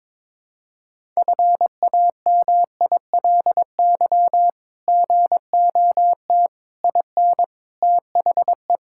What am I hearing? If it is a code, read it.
Morse code, heard as a keyed tone, about 21 wpm, 700 Hz. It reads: FAMILY GOT IN THE
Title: Sentences from Top 1000 Words - Review Part 6 - 22wpm